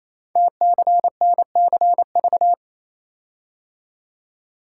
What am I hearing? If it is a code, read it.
Morse code, heard as a keyed tone, about 29 wpm, 690 Hz. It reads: TCNCV
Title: Code Groups - 5 Letters - 28wpm